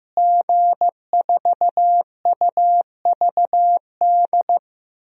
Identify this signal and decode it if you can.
Morse code, heard as a keyed tone, about 15 wpm, 700 Hz. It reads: G4UVD
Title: DXCC-100 Call Signs - 5 Characters - Repeated 2x - 15wpm